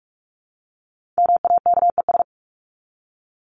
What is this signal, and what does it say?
Morse code, heard as a keyed tone, about 44 wpm, 700 Hz. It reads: MAKES
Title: ES:S Suffix - Top 100 Words with Base Words 45wpm